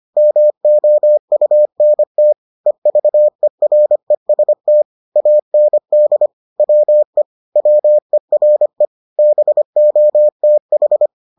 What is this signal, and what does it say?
Morse code, heard as a keyed tone, about 25 wpm, 600 Hz. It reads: MOUNT EVEREST AND WE WERE BOTH